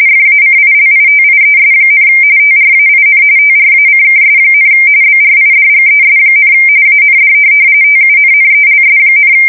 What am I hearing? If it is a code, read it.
RTTY, about 45.45 baud. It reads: FACILITIES, TRAIN TRACKS, BRIDGES, AND TUNNELS,
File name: fsk170 / USS-New-Jersey